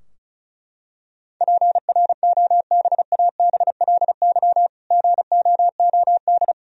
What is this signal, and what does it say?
Morse code, heard as a keyed tone, about 36 wpm, 700 Hz. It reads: PROBABLY GOOD
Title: Sentences from Top 1000 Words - Review Part 4 - 35wpm